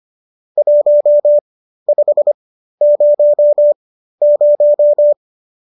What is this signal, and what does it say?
Morse code, heard as a keyed tone, about 25 wpm, spaced as for 13 wpm, 590 Hz. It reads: 1500